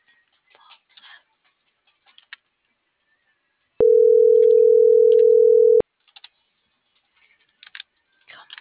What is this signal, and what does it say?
Ringback tone (North America), 440 + 480 Hz.
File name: CENTRAL NMP / out melanie.diaznmp - .41454